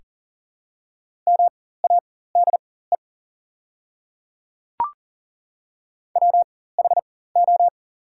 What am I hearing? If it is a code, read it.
Morse code, heard as a keyed tone, about 40 wpm, spaced as for 19 wpm, 700 Hz. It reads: MADE WHO
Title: 4x Character Spacing - Top 200 Words - 40wpm